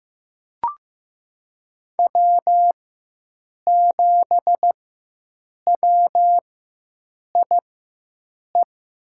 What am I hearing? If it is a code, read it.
Morse code, heard as a keyed tone, about 15 wpm, spaced as for 7 wpm, 700 Hz. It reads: W7WIE
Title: USA Call Signs - 5 Characters - 4x Character Spacing - 15wpm